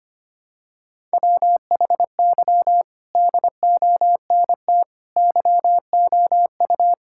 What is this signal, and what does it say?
Morse code, heard as a keyed tone, about 25 wpm, 700 Hz. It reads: WHY DONT YOU
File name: practice / 25wpm / Sentences with 4-Word Phrases - 25wpm